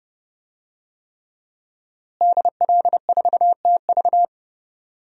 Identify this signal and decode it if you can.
Morse code, heard as a keyed tone, about 28 wpm, 700 Hz. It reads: DL4TV
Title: DXCC-100 Call Signs - 5 Characters - Repeated 3x - 30wpm